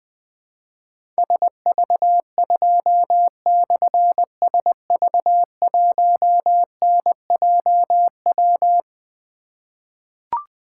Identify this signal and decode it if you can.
Morse code, heard as a keyed tone, about 21 wpm, 700 Hz. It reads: SV2/SV1NJW
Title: Difficult Call Signs 20wpm